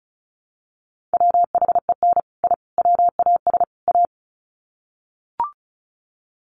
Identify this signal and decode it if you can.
Morse code, heard as a keyed tone, about 34 wpm, 700 Hz. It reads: WHEN I WAS A
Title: Sets of 4 Words 35wpm